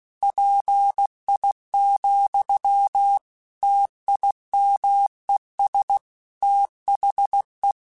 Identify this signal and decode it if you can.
Morse code, heard as a keyed tone, about 16 wpm, 800 Hz. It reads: PI, TIMES THE
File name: Technical Subjects / Technical Subjects-mp3s / XC @15